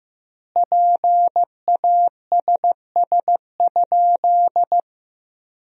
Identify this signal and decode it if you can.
Morse code, heard as a keyed tone, about 15 wpm, 700 Hz. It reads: PASS?
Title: 2x Word-Spacing - Sentences from Top 400 Words - 15wpm